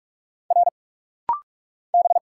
Morse code, heard as a keyed tone, about 46 wpm, 700 Hz. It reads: R B
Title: Single Letters - Rapid-Fire 45wpm